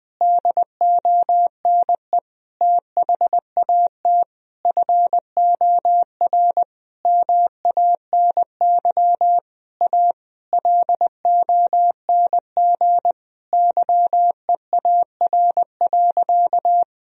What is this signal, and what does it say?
Morse code, heard as a keyed tone, about 20 wpm, 700 Hz. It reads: DONE THAT FOR MANY A LONG YEAR.
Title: Sentences from Top 1000 Words - Review Part 1 - 20wpm